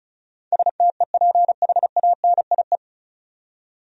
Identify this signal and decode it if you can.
Morse code, heard as a keyed tone, about 36 wpm, 700 Hz. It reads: STEPHANIE
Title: USA Common Names - 35wpm